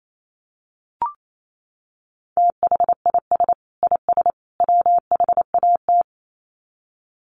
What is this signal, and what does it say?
Morse code, heard as a keyed tone, about 28 wpm, 700 Hz. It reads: THIS IS WHAT
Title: Sets of 3 Words 28wpm